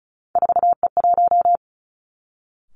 Morse code, heard as a keyed tone, about 36 wpm, 700 Hz. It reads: 4E1
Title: Three Challenging Letter-Number Combinations 35wpm